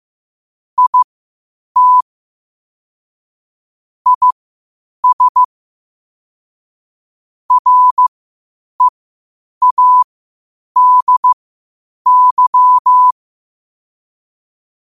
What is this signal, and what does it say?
Morse code, heard as a keyed tone, about 15 wpm, spaced as for 8 wpm, 1000 Hz. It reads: IT IS READY
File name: qotdfiles / 8wpm